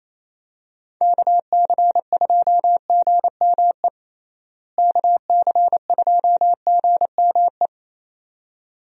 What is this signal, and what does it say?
Morse code, heard as a keyed tone, about 28 wpm, 700 Hz. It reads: KC2GME KC2GME
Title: USA Call Signs - 6 Characters - Repeated 2x - 28wpm